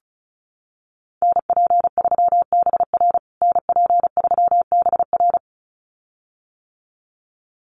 Morse code, heard as a keyed tone, about 34 wpm, 700 Hz. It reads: NP3BR NP3BR
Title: USA Call Signs Repeated 2x 35wpm